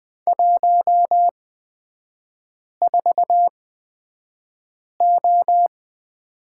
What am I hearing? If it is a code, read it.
Morse code, heard as a keyed tone, about 20 wpm, spaced as for 5 wpm, 700 Hz. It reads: 14O
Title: 250 Lesson 40e - BK - Three Characters 20wpm